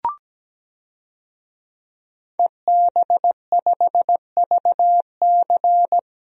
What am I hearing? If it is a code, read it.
Morse code, heard as a keyed tone, about 17 wpm, 700 Hz. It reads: EB5VC
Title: DXCC-100 Call Signs - 5 Characters - 17wpm